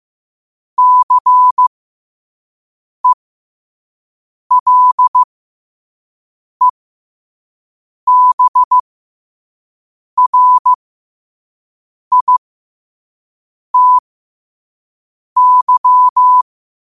Morse code, heard as a keyed tone, about 15 wpm, spaced as for 5 wpm, 1000 Hz. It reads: CELEBRITY
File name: 5wpm